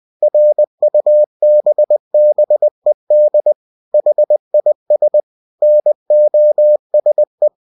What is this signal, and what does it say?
Morse code, heard as a keyed tone, about 20 wpm, 600 Hz. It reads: RUBBED HIS NOSE
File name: winnie-20wpm